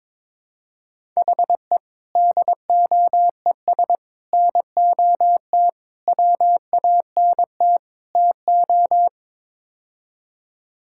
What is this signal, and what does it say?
Morse code, heard as a keyed tone, about 22 wpm, 700 Hz. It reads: HE DOES NOT WANT TO
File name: practice / 22wpm / Build A Phrase - 5-Word Phrases - 22wpm